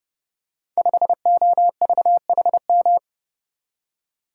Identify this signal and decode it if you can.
Morse code, heard as a keyed tone, about 30 wpm, 700 Hz. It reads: 5OVHM
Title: Code Groups - 5 Letter:Numbers - 30wpm